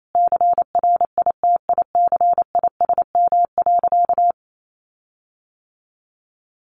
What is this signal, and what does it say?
Morse code, heard as a keyed tone, about 28 wpm, 700 Hz. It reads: CRITICISM.
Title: Sentences from Top 1300 Words 28wpm